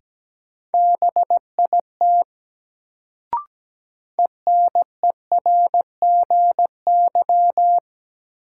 Morse code, heard as a keyed tone, about 17 wpm, 700 Hz. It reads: BIT ENERGY
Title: Sentences from Top 900 Words - New Word Drill - 17wpm